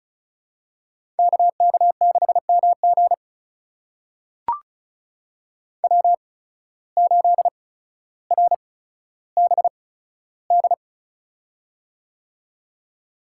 Morse code, heard as a keyed tone, about 36 wpm, spaced as for 15 wpm, 700 Hz. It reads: KK6MG W8RBD
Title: USA Call Signs - 5 Characters - 8x Character Spacing - 35wpm